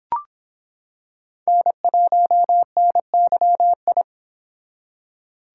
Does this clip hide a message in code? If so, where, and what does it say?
Yes: Morse code from 1.5 to 4.0 s, heard as a keyed tone, about 25 wpm, 700 Hz. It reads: N1NYI